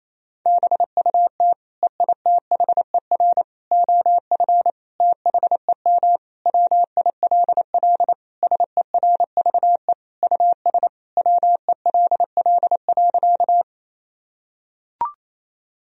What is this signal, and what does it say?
Morse code, heard as a keyed tone, about 28 wpm, 700 Hz. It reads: BUT EITHER OF THEM WILL SERVE US WELL.
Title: Sentences from Top 1000 Words - Review Part 4 - 28wpm